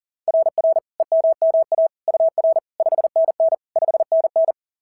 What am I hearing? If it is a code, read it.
Morse code, heard as a keyed tone, about 40 wpm, 690 Hz. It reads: RR EMMA UR 5NN 5NN
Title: SOTA QSO Practice 40wpm